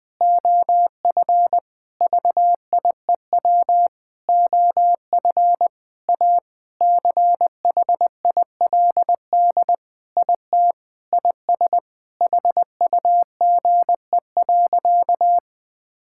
Morse code, heard as a keyed tone, about 20 wpm, 700 Hz. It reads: OF VIEW OF A CHILD IT IS HUGE.
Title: Sentences with 5-Word Phrases - 20wpm